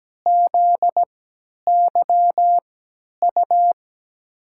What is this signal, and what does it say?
Morse code, heard as a keyed tone, about 17 wpm, spaced as for 10 wpm, 700 Hz. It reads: ZYU
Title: USA Call Signs - 5 Characters - 3x Character Spacing - 17wpm